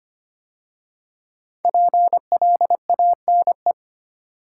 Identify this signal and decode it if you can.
Morse code, heard as a keyed tone, about 25 wpm, 700 Hz. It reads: PLANE
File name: 5 Letter Words - First 3 Letters Given - 25wpm